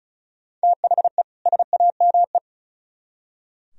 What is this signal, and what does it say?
Morse code, heard as a keyed tone, about 44 wpm, 700 Hz. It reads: THE SAME